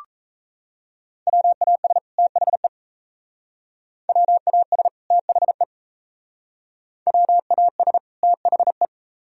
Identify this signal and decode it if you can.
Morse code, heard as a keyed tone, about 36 wpm, 700 Hz. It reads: WAS THE WAS THE WAS THE